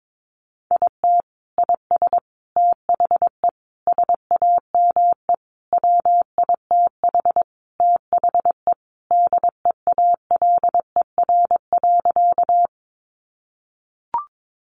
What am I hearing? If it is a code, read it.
Morse code, heard as a keyed tone, about 22 wpm, 700 Hz. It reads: IT IS THE SAME WITH THE DEALER.
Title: Sentences from Top 1100 Words 22wpm